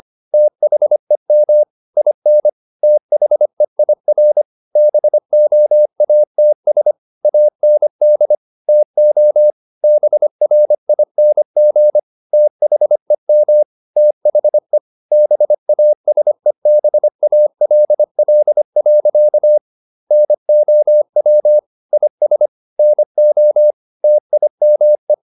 Morse code, heard as a keyed tone, about 25 wpm, 600 Hz. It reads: THEM IN THEIR BOATS AND TO BRING THEM THE BASEBALL. NOW IS NO TIME
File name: the old man and the sea part 1 25wpm